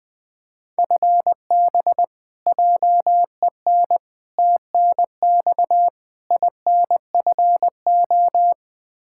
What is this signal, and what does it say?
Morse code, heard as a keyed tone, about 21 wpm, 700 Hz. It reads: FB JEN TNX INFO
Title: Practice QSO - Exchange 3 - More Info - 20wpm